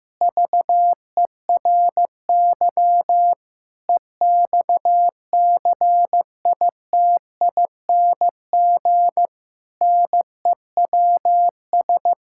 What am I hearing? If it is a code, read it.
Morse code, heard as a keyed tone, about 15 wpm, 700 Hz. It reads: VERY EXCITING NEWS
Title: Pangram Challenge v2 - 15wpm